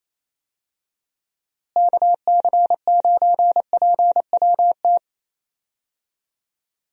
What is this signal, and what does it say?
Morse code, heard as a keyed tone, about 28 wpm, 700 Hz. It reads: KC9PWT
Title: USA Call Signs - 6 Characters - Repeated 2x - 28wpm